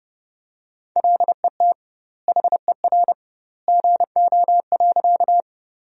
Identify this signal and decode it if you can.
Morse code, heard as a keyed tone, about 30 wpm, 700 Hz. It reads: LET HER GO.
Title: 2x Word-Spacing - Sentences from Top 300 Words - 30wpm